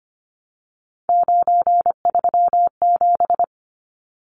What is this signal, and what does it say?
Morse code, heard as a keyed tone, about 25 wpm, 700 Hz. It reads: 937
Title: Numbers - 3-Digit 25wpm